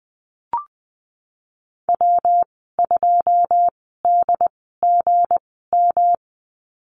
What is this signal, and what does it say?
Morse code, heard as a keyed tone, about 19 wpm, 700 Hz. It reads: W 2 D G M
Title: 231 Lesson 37g - Letter x - Call Signs 20wpm